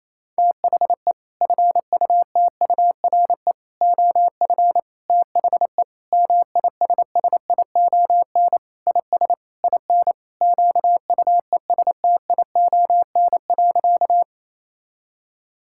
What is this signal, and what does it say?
Morse code, heard as a keyed tone, about 28 wpm, 700 Hz. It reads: THE FUTURE OF THE MISSION IS IN QUESTION.
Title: Sentences from Top 1600 Words 28wpm